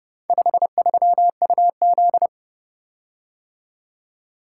Morse code, heard as a keyed tone, about 28 wpm, 700 Hz. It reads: 53UZ